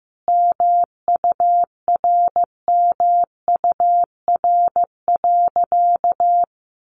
Morse code, heard as a keyed tone, about 15 wpm, 700 Hz. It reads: MURMUR.
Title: A Sound Of Thunder 15wpm